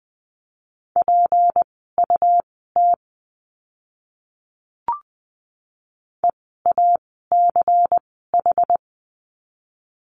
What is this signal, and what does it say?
Morse code, heard as a keyed tone, about 20 wpm, spaced as for 15 wpm, 700 Hz. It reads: PUT EACH
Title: 230 Lesson 37f - Letter x - Words 20wpm